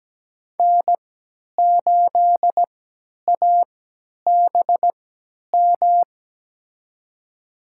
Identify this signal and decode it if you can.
Morse code, heard as a keyed tone, about 17 wpm, spaced as for 10 wpm, 700 Hz. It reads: N8ABM
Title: USA Call Signs - 5 Characters - 3x Character Spacing - 17wpm